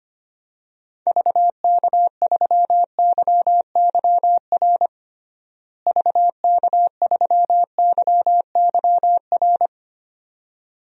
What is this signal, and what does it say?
Morse code, heard as a keyed tone, about 25 wpm, 700 Hz. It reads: VK3YYR VK3YYR
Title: DXCC-100 Call Signs - 6 Characters - Repeated 2x - 25wpm